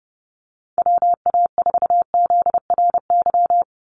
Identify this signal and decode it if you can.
Morse code, heard as a keyed tone, about 30 wpm, 700 Hz. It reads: WA4ZRY
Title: USA Call Signs 30wpm